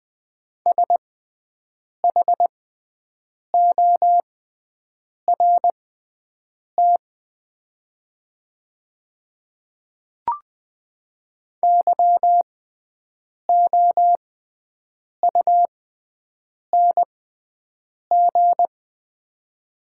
Morse code, heard as a keyed tone, about 20 wpm, spaced as for 7 wpm, 700 Hz. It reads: SHORT YOUNG
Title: 6x Character Spacing - Top 500 Words - 20wpm